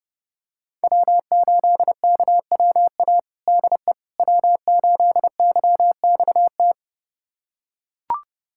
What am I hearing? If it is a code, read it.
Morse code, heard as a keyed tone, about 30 wpm, 700 Hz. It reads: W8KWA DE W8YXT
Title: USA Callsign DE Callsign - 5 Characters - 30wpm